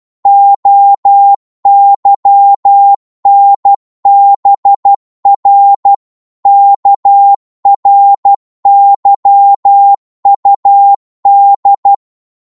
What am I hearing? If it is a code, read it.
Morse code, heard as a keyed tone, about 12 wpm, 800 Hz. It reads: OYNBR KRYUD
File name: cwmp3 / ltr-12wpm